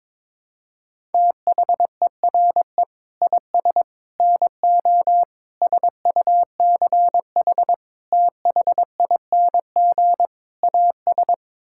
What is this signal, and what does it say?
Morse code, heard as a keyed tone, about 22 wpm, 700 Hz. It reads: THERE IS NO SUCH THING AS
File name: practice / 22wpm / Sets of 6 Words - 22wpm